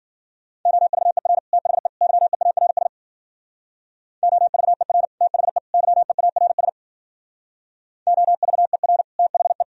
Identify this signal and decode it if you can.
Morse code, heard as a keyed tone, about 51 wpm, 700 Hz. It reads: OVER THE YEARS OVER THE YEARS OVER THE